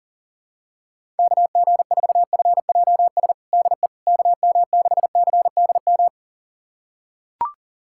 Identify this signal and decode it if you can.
Morse code, heard as a keyed tone, about 37 wpm, 700 Hz. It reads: KG4FJS DE KM6CDM
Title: USA Callsign DE Callsign 40wpm